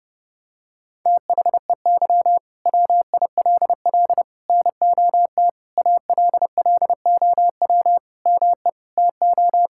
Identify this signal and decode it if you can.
Morse code, heard as a keyed tone, about 32 wpm, 700 Hz. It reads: THEY WILL NOT ALLOW ME TO
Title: Sentences from Top 1000 Words - Review Part 5 - 30wpm